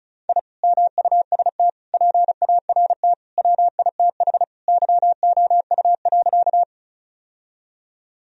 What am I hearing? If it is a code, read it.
Morse code, heard as a keyed tone, about 36 wpm, 700 Hz. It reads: I MUST PART WITH YOU.
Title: Sentences from Top 1000 Words - Review Part 1 - 35wpm